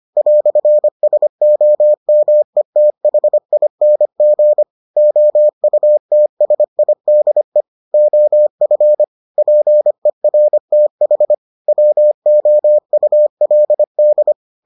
Morse code, heard as a keyed tone, about 25 wpm, 600 Hz. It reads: "SOMETHING OUTSIDE OF PERTH WOULD